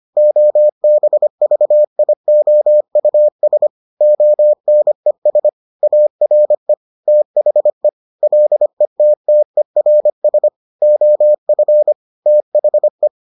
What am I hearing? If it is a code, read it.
Morse code, heard as a keyed tone, about 25 wpm, 600 Hz. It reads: OBVIOUS ONES ARE THE LETTERS OF THE